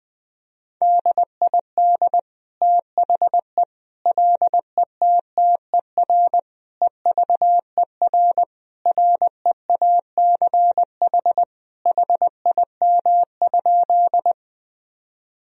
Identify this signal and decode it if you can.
Morse code, heard as a keyed tone, about 20 wpm, 700 Hz. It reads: DID THE LETTER EVER REACH HIM?
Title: Sentences from Top 1000 Words - Review Part 2 - 20wpm